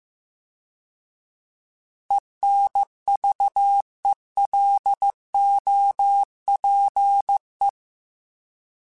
Morse code, heard as a keyed tone, about 15 wpm, 800 Hz. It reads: ENVELOPE
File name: Alphabetical / Alphabetical mp3s / Es @13